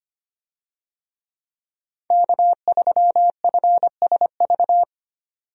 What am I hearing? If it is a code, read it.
Morse code, heard as a keyed tone, about 25 wpm, 700 Hz. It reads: K3FSV
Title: USA Call Signs - 5 Characters - 25wpm